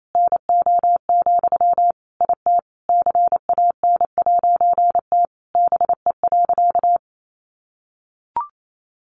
Morse code, heard as a keyed tone, about 28 wpm, 700 Hz. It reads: NO, IT CAN'T BE.
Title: A Sound Of Thunder 28wpm